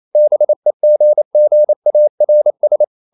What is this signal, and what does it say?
Morse code, heard as a keyed tone, about 28 wpm, 600 Hz. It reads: BEGGARS
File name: CW / Practice / EnglishSayings28